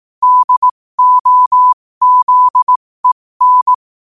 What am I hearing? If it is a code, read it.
Morse code, heard as a keyed tone, about 18 wpm, spaced as for 15 wpm, 1000 Hz. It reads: DOZEN
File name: qotdfiles / 15wpm